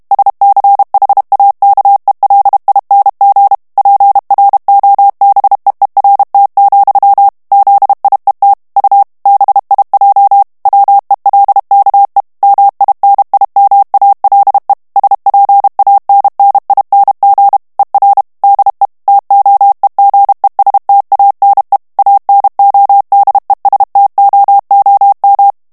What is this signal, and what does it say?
Morse code, heard as a keyed tone, about 32 wpm, 800 Hz. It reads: SCHAKELING PROBEERT, ZIET U BIJ WELKE MINIMALE SPANNING ER DE TOEGESTANE ANODESTOOM
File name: HOLLANDAIS texte clair 2 vitesse 30 mots minute